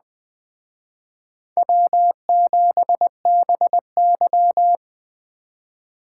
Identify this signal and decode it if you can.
Morse code, heard as a keyed tone, about 19 wpm, 700 Hz. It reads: W7BY